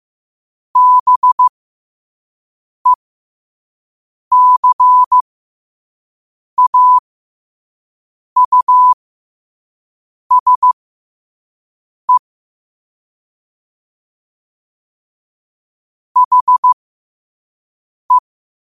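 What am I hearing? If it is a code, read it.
Morse code, heard as a keyed tone, about 15 wpm, spaced as for 5 wpm, 1000 Hz. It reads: BECAUSE HE
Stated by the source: Morse at 5 wpm.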